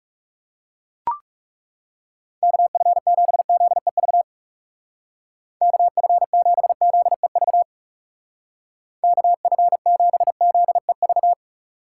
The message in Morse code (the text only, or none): KF7ZEV KF7ZEV KF7ZEV